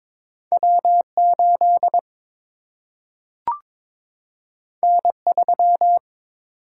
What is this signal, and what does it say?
Morse code, heard as a keyed tone, about 22 wpm, 700 Hz. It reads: W 8 N 3
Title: Single-Letter Single-Number 22wpm